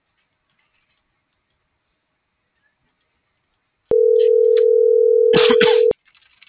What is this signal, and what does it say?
Ringback tone (North America), 440 + 480 Hz.